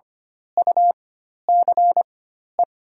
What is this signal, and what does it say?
Morse code, heard as a keyed tone, about 25 wpm, spaced as for 12 wpm, 690 Hz. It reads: UCE
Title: 4x Character Spacing - Top 500 Words - 25wpm